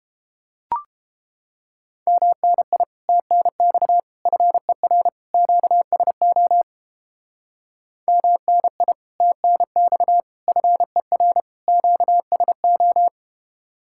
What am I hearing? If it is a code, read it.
Morse code, heard as a keyed tone, about 29 wpm, 700 Hz. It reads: MNI TNX FER QSO MNI TNX FER QSO